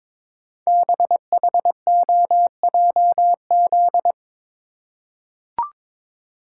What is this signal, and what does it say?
Morse code, heard as a keyed tone, about 22 wpm, 700 Hz. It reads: BHOJZ
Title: Code Groups - 5 Letters - 22wpm